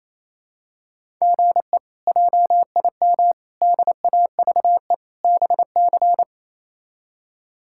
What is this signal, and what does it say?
Morse code, heard as a keyed tone, about 28 wpm, 700 Hz. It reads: GE JIM DAVE BC
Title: K1USN SST QSO Practice - 28wpm